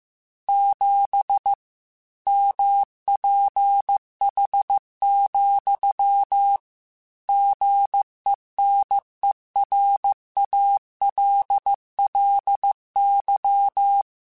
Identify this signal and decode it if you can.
Morse code, heard as a keyed tone, about 15 wpm, 800 Hz. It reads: 7 MPH, GENERALLY
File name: Weather Sets / Weather Sets-mp3s / Weather 4 24 @13